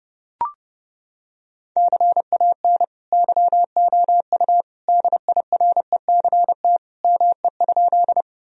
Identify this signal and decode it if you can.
Morse code, heard as a keyed tone, about 30 wpm, 700 Hz. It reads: CAN YOU DIRECT ME?
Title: Sentences from Top 1000 Words - Review Part 2 - 30wpm